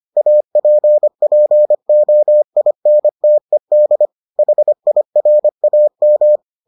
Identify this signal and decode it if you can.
Morse code, heard as a keyed tone, about 25 wpm, 600 Hz. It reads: APPOINTED HIRAM